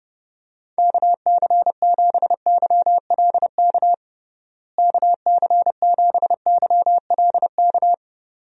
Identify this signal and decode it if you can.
Morse code, heard as a keyed tone, about 30 wpm, 700 Hz. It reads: KC7YLK KC7YLK
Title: USA Call Signs - 6 Characters - Repeated 2x - 30wpm